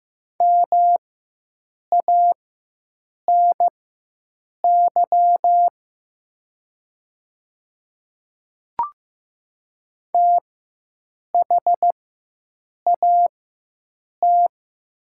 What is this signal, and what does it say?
Morse code, heard as a keyed tone, about 15 wpm, spaced as for 7 wpm, 700 Hz. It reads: MANY THAT
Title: 4x Character Spacing - Top 300 Words - 15wpm